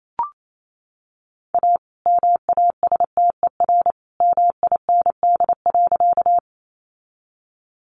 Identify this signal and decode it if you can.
Morse code, heard as a keyed tone, about 28 wpm, 700 Hz. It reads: A MASTER MIND.
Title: Sentences from Top 1300 Words 28wpm